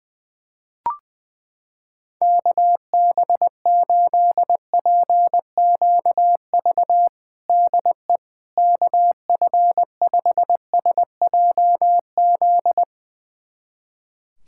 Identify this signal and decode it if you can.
Morse code, heard as a keyed tone, about 21 wpm, 700 Hz. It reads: KB8PQV DE KF5SJZ
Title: USA Callsign DE Callsign - 6 Characters - 20wpm